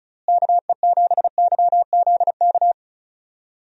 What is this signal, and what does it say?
Morse code, heard as a keyed tone, about 34 wpm, 700 Hz. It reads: KE7YZK
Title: USA Call Signs - 6 Characters - Repeated 2x - 35wpm